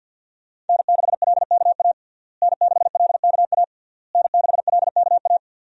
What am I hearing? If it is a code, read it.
Morse code, heard as a keyed tone, about 47 wpm, 700 Hz. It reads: N6LKA N6LKA N6LKA
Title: USA Call Signs - 5 Characters - Repeated 3x - 50wpm